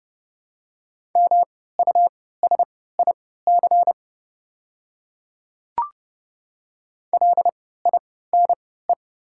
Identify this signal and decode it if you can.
Morse code, heard as a keyed tone, about 30 wpm, spaced as for 17 wpm, 700 Hz. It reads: MUSIC LINE
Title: 3x Character Spacing - Top 300 Words - 30wpm